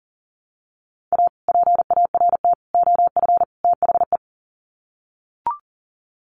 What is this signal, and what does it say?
Morse code, heard as a keyed tone, about 37 wpm, 700 Hz. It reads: A PART OF THE
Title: Sets of 4 Words 40wpm